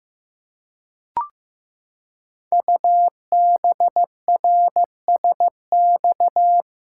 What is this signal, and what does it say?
Morse code, heard as a keyed tone, about 15 wpm, 700 Hz. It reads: UBRSX
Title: Code Groups - 5 Letters - 15wpm